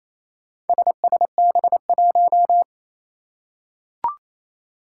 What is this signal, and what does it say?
Morse code, heard as a keyed tone, about 28 wpm, 700 Hz. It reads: SSB1
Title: Four Challenging Letter-Number Combinations 28wpm